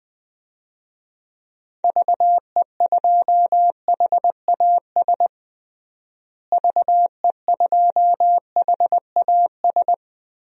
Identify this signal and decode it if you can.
Morse code, heard as a keyed tone, about 20 wpm, 700 Hz. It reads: VE2HAS VE2HAS
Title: DXCC-100 Call Signs - 6 Characters - Repeated 3x - 20wpm